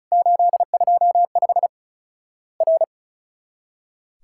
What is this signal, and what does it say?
Morse code, heard as a keyed tone, about 34 wpm, 700 Hz. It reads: 825 R
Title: Worked All Europe DX Contest - QTC Exchange - 35wpm